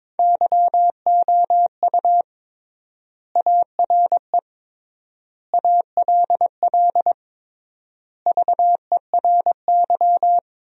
Morse code, heard as a keyed tone, about 22 wpm, 700 Hz. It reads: YOU ARE ALL VERY
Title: 3x Word-Spacing - Sentences from Top 200 Words - 22wpm